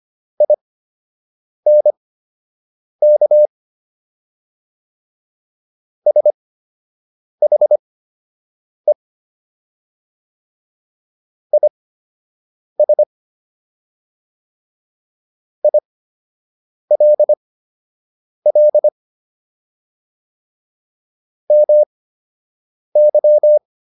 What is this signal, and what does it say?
Morse code, heard as a keyed tone, about 25 wpm, spaced as for 7 wpm, 600 Hz. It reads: INK SHE IS ILL MY